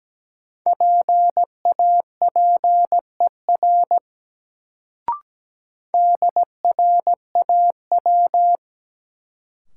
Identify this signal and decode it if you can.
Morse code, heard as a keyed tone, about 17 wpm, 700 Hz. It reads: PAPER DRAW
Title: Sentences from Top 300 Words - New Word Drill - 17wpm